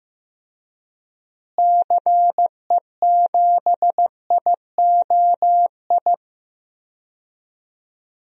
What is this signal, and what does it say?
Morse code, heard as a keyed tone, about 15 wpm, 700 Hz. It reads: CE7IOI
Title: DXCC-100 Call Signs - 6 Characters - Repeated 3x - 15wpm